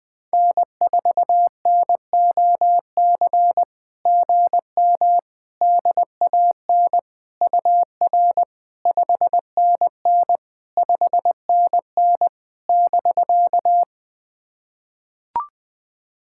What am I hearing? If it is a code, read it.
Morse code, heard as a keyed tone, about 21 wpm, 700 Hz. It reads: N4NOC GM DAN UR 5NN 5NN <BK>
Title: SOTA QSO Practice 20wpm